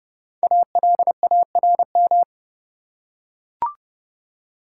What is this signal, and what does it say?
Morse code, heard as a keyed tone, about 30 wpm, 700 Hz. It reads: ALARM
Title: Five Letter Words - 30wpm